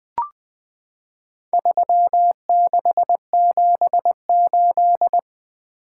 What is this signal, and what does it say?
Morse code, heard as a keyed tone, about 20 wpm, 700 Hz. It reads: 3678
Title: Four Letter-Number Combinations - 20wpm